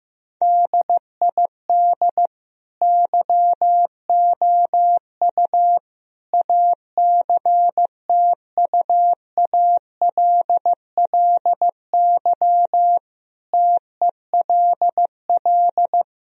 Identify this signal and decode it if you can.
Morse code, heard as a keyed tone, about 16 wpm, 700 Hz. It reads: DID YOU ACTUALLY TELL
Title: Sentences from Top 1000 Words - Review Part 4 - 15wpm